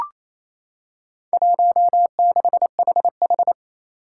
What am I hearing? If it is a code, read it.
Morse code, heard as a keyed tone, about 28 wpm, 700 Hz. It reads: 16HH
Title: Four Challenging Letter-Number Combinations 28wpm